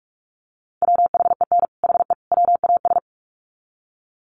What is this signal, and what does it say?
Morse code, heard as a keyed tone, about 44 wpm, 700 Hz. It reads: WHEN HE WAS
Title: Sets of 3 Words 45wpm